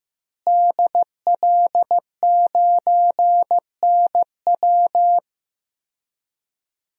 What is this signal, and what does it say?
Morse code, heard as a keyed tone, about 15 wpm, 700 Hz. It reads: DL9NW
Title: DXCC-100 Call Signs - 5 Characters - Repeated 2x - 15wpm